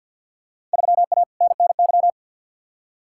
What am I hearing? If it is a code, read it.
Morse code, heard as a keyed tone, about 52 wpm, 700 Hz. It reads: 3A NNY
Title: Field Day Exchanges - 50wpm